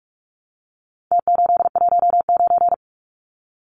Morse code, heard as a keyed tone, about 44 wpm, 700 Hz. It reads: T819
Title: Numbers - Contest Style - 4-Digit 45wpm